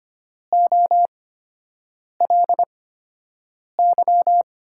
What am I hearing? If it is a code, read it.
Morse code, heard as a keyed tone, about 25 wpm, spaced as for 7 wpm, 700 Hz. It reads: OLY
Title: USA Call Signs - 5 Characters - 8x Character Spacing - 25wpm